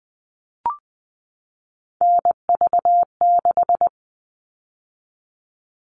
Morse code, heard as a keyed tone, about 20 wpm, 700 Hz. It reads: NV6
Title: All Cut Numbers - 3-Digit 20wpm